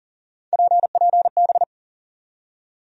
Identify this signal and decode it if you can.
Morse code, heard as a keyed tone, about 39 wpm, 700 Hz. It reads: PPB
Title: USA Call Sign Postfixes - 3 Letters 40wpm